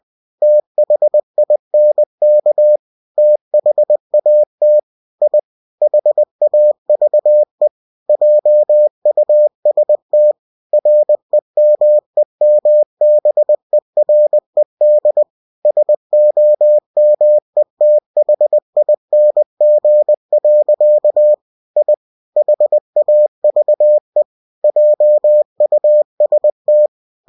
Morse code, heard as a keyed tone, about 20 wpm, 600 Hz. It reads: THINK THAT I HAVE JUST REMEMBERED SOMETHING. I HAVE JUST
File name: cw-books / winnie-the-pooh / winnie-20wpm-page-0024